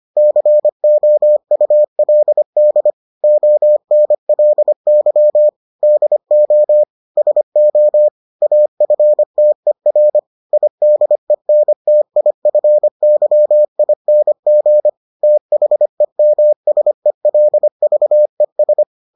Morse code, heard as a keyed tone, about 25 wpm, 600 Hz. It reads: COULD ONLY DO SO AFTER IDENTIFYING THEMSELVES